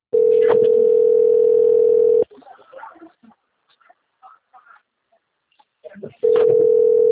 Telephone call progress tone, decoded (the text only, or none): ringback tone